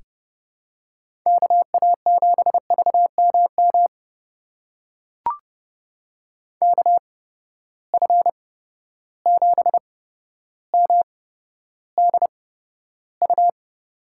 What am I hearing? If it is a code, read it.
Morse code, heard as a keyed tone, about 32 wpm, spaced as for 13 wpm, 700 Hz. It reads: KA7VMM KF7MDU
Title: USA Call Signs - 6 Characters - 8x Character Spacing - 30wpm